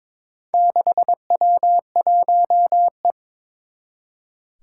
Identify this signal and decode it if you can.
Morse code, heard as a keyed tone, about 22 wpm, 700 Hz. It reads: 6W1E
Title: Four Challenging Letter-Number Combinations 22wpm